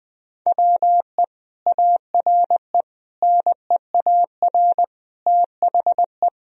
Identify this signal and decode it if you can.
Morse code, heard as a keyed tone, about 19 wpm, 700 Hz. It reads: WE ARE NEAR THE
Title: Sentences from Top 1000 Words - Review Part 1 - 20wpm